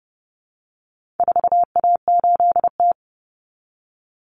Morse code, heard as a keyed tone, about 30 wpm, 700 Hz. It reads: 4A8T
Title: Common Cut Numbers - 4-Digit 30wpm